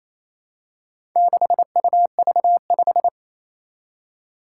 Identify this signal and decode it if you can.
Morse code, heard as a keyed tone, about 28 wpm, 700 Hz. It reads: 6UV5
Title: Four Challenging Letter-Number Combinations 28wpm